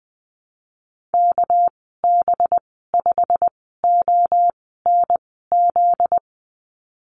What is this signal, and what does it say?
Morse code, heard as a keyed tone, about 20 wpm, spaced as for 15 wpm, 700 Hz. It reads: KB5ONZ
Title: 245 Lesson 39g - Letter z - Call Signs 20wpm